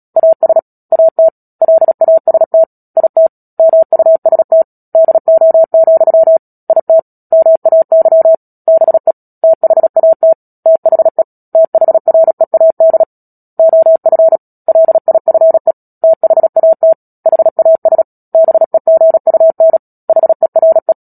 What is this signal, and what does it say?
Morse code, heard as a keyed tone, about 37 wpm, 650 Hz. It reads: AS AT LAST IT MUST DO, IT MAY BE THAT THE THREAD OF LIFE THAT HAS BEGUN HERE
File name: Book Two / 10-Chapter 10 - Epilogue - 36 WPM